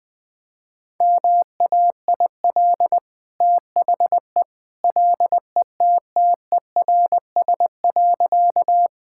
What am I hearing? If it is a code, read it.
Morse code, heard as a keyed tone, about 21 wpm, 700 Hz. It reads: MAIL THE LETTERS.
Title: Easy Sentences with Hints - 20wpm